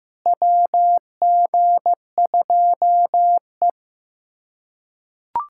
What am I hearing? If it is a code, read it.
Morse code, heard as a keyed tone, about 15 wpm, 700 Hz. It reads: WG2E
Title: USA Callsign DE Callsign - 4 Characters - 15wpm